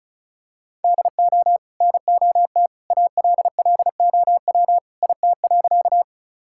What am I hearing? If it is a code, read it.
Morse code, heard as a keyed tone, about 34 wpm, 700 Hz. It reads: DO NOT ALLOW IT.
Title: Sentences from Top 1000 Words - Review Part 5 - 35wpm